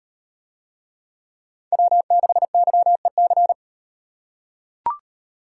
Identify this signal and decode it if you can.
Morse code, heard as a keyed tone, about 38 wpm, 700 Hz. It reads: W6YEC